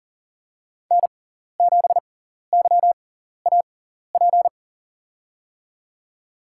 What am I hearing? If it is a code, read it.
Morse code, heard as a keyed tone, about 40 wpm, spaced as for 14 wpm, 700 Hz. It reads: N7YAP
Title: USA Call Signs - 6x Character Spacing - 40wpm